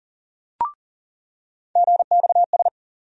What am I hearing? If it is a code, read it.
Morse code, heard as a keyed tone, about 40 wpm, 700 Hz. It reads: GXS
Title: USA Call Sign Postfixes - 3 Letters 40wpm